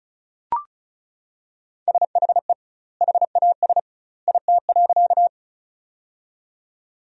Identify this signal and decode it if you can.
Morse code, heard as a keyed tone, about 34 wpm, 700 Hz. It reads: SHE HAS IT.
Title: 2x Word-Spacing - Sentences from Top 100 Words - 35wpm